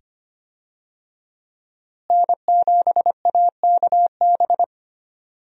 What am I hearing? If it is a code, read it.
Morse code, heard as a keyed tone, about 25 wpm, 700 Hz. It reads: N7AKB
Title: USA Call Signs - 5 Characters - Repeated 2x - 25wpm